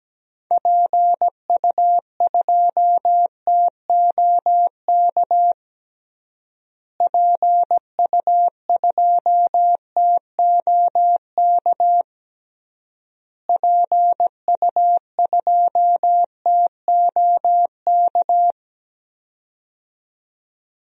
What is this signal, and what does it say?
Morse code, heard as a keyed tone, about 17 wpm, 700 Hz. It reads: PU2TOK PU2TOK PU2TOK
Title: DXCC-100 Call Signs - 6 Characters - Repeated 3x - 17wpm